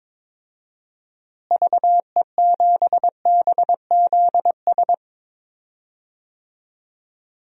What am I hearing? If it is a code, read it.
Morse code, heard as a keyed tone, about 22 wpm, 700 Hz. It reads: VE7BZS
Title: DXCC-100 Call Signs - 6 Characters - Repeated 2x - 22wpm